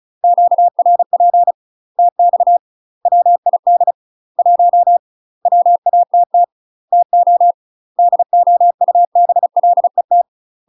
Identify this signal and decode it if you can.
Morse code, heard as a keyed tone, about 36 wpm, 700 Hz. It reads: QRP TX WID 1 WATT TO DOUBLET
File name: UK MASTER QSOs TRAINING / 35WPM / UK MASTER QSOs 0022